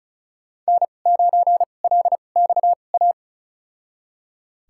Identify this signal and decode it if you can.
Morse code, heard as a keyed tone, about 36 wpm, spaced as for 25 wpm, 700 Hz. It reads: N9LXA
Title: USA Call Signs - 5 Characters - 2x Character Spacing - 35wpm